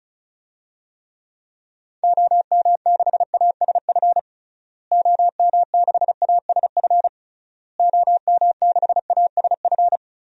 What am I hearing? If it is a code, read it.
Morse code, heard as a keyed tone, about 36 wpm, 700 Hz. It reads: OM6ASF OM6ASF OM6ASF
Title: DXCC-100 Call Signs - 6 Characters - Repeated 3x - 35wpm